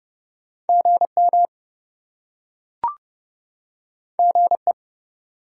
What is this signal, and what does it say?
Morse code, heard as a keyed tone, about 28 wpm, 700 Hz. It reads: GM GE